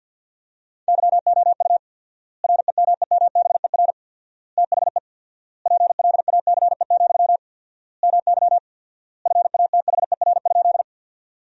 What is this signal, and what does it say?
Morse code, heard as a keyed tone, about 50 wpm, 700 Hz. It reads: YOU REMEMBER THE PLACE, MY FATHER?
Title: 4x Word-Spacing - Sentences from Top 400 Words - 50wpm